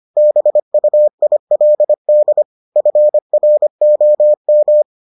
Morse code, heard as a keyed tone, about 25 wpm, 600 Hz. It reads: BUILD FROM